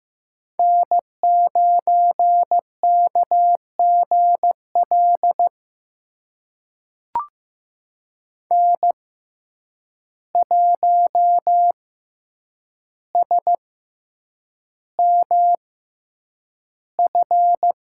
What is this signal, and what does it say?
Morse code, heard as a keyed tone, about 15 wpm, spaced as for 8 wpm, 700 Hz. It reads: N9KGL N1SMF
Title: USA Call Signs - 6x Character Spacing - 15wpm